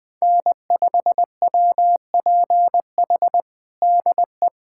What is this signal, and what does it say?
Morse code, heard as a keyed tone, about 21 wpm, 700 Hz. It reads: N5WPH DE
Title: USA Callsign DE Callsign 20wpm